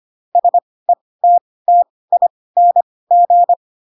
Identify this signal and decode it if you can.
Morse code, heard as a keyed tone, about 25 wpm, spaced as for 18 wpm, 700 Hz. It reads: SETTING
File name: Pinocchio ch3 12 25 ik1hge